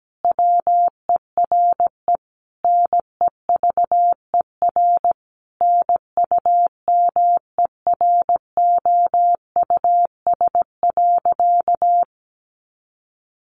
Morse code, heard as a keyed tone, about 17 wpm, 700 Hz. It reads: WERE NEVER NUMEROUS.
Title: Sentences from Top 1400 Words 17wpm